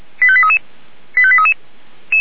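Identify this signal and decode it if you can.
Five-tone selective call (ZVEI), heard as quick runs of five tones, tones of about 70 ms. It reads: 85833 85833
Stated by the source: ZVEI five-tone selective call received on SDR.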